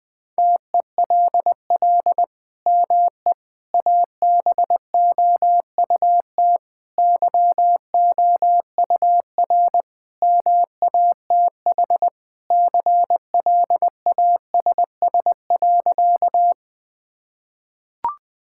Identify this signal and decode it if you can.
Morse code, heard as a keyed tone, about 20 wpm, 700 Hz. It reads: TELL ME ABOUT YOUR MATH CLASS.
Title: Sentences from Top 1000 Words - Review Part 6 - 20wpm